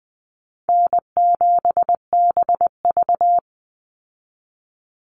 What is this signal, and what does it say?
Morse code, heard as a keyed tone, about 20 wpm, 700 Hz. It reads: N7BV
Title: Call Signs - 20wpm